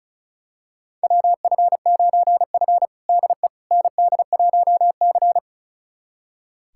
Morse code, heard as a keyed tone, about 36 wpm, 700 Hz. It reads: WF9F DE ND1C
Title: USA Callsign DE Callsign - 4 Characters - 35wpm